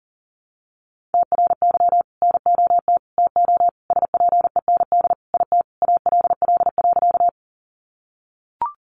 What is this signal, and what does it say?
Morse code, heard as a keyed tone, about 40 wpm, 700 Hz. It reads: TRY NOT TO SPEND IT ALL.
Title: Sentences from Top 800 Words 40wpm